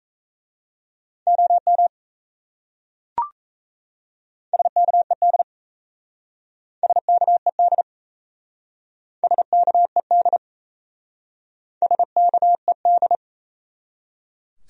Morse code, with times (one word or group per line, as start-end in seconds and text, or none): 1.3-1.9 OM
4.5-5.4 SKED
6.8-7.8 SKED
9.2-10.4 SKED
11.8-13.2 SKED